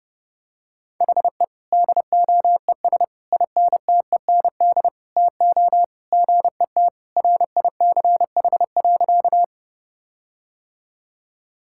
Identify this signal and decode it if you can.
Morse code, heard as a keyed tone, about 32 wpm, 700 Hz. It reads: HE DOES INTEND TO GET RICH.
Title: Sentences from Top 1600 Words 30wpm